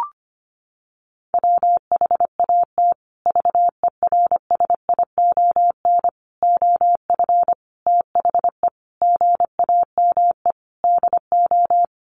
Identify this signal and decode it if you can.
Morse code, heard as a keyed tone, about 25 wpm, 700 Hz. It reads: WHAT VERSION OF THE GAME DO
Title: Sentences from Top 1400 Words 25wpm